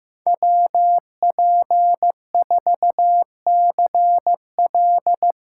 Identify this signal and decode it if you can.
Morse code, heard as a keyed tone, about 16 wpm, 700 Hz. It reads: WP4CL
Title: USA Callsign DE Callsign - 5 Characters - 15wpm